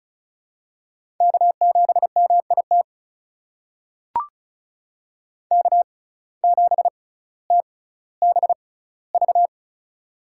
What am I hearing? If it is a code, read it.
Morse code, heard as a keyed tone, about 36 wpm, spaced as for 18 wpm, 700 Hz. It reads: K7MIT K7TBV
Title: USA Call Signs - 5 Characters - 6x Character Spacing - 35wpm